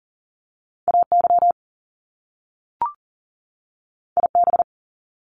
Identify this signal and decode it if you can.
Morse code, heard as a keyed tone, about 39 wpm, 700 Hz. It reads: AY IB
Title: Top Two Letter Combinations in Words 40wpm